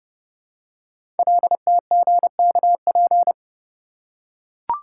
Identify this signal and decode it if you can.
Morse code, heard as a keyed tone, about 30 wpm, 700 Hz. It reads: LTGKP
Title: Code Groups - 5 Letter:Numbers - 30wpm